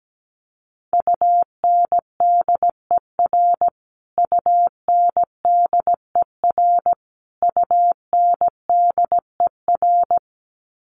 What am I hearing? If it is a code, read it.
Morse code, heard as a keyed tone, about 17 wpm, 700 Hz. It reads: UNDER UNDER UNDER
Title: Top 500 Words Repeated 17wpm